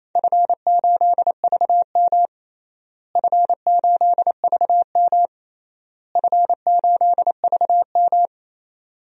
Morse code, heard as a keyed tone, about 28 wpm, 700 Hz. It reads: F8VM F8VM F8VM
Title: DXCC-100 Call Signs - 4 Characters - Repeated 3x - 28wpm